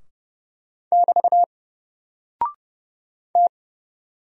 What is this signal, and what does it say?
Morse code, heard as a keyed tone, about 30 wpm, 700 Hz. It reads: <BT> T
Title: Single Character (Letter:Number:Punctuation:Prosign) Alt 30wpm